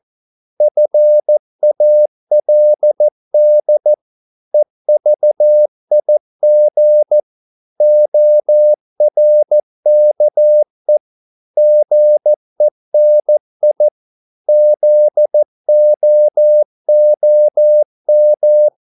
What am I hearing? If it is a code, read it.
Morse code, heard as a keyed tone, about 14 wpm, 600 Hz. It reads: FALD EVIG ORKE GENI ZOOM
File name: Korte ord DK 14wpm